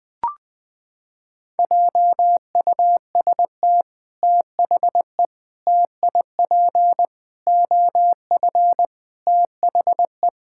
Morse code, heard as a keyed tone, about 21 wpm, 700 Hz. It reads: JUST THE TIP OF THE
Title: Sets of 6 Words - 20wpm